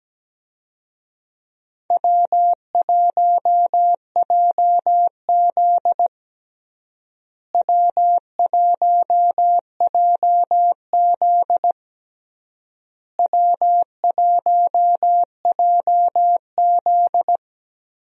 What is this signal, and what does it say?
Morse code, heard as a keyed tone, about 17 wpm, 700 Hz. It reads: W1JZ W1JZ W1JZ
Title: USA Call Signs - 4 Characters - Repeated 3x - 17wpm